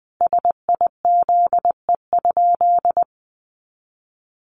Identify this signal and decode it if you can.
Morse code, heard as a keyed tone, about 20 wpm, 700 Hz. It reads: SIZE?
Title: Sentences from Top 500 Words 20wpm